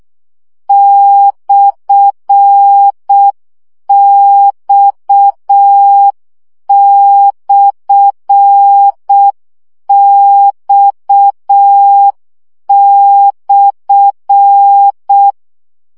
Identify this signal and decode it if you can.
Morse code, heard as a keyed tone, about 6 wpm, 800 Hz. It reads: /X/X/
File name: CW gyakorlas x per 5 baud